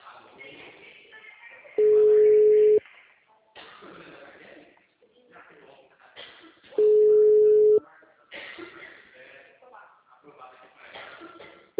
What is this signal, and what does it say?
Ringback tone (Europe, 425 Hz), with speech.
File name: out- -2002- - - .79025